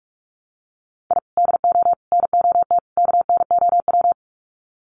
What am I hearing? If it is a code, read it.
Morse code, heard as a keyed tone, about 44 wpm, 700 Hz. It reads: I DO NOT KNOW
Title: Sets of 4 Words 45wpm